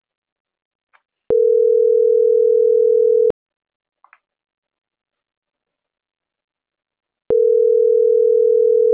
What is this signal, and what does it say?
Ringback tone (North America), 440 + 480 Hz.
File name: CENTRAL NAVIERAS / out roberto.rueda